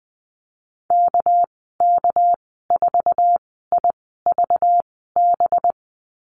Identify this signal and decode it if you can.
Morse code, heard as a keyed tone, about 19 wpm, spaced as for 14 wpm, 700 Hz. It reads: KK4IVB
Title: 213 Lesson 34g - Number 8 - Call Signs 20wpm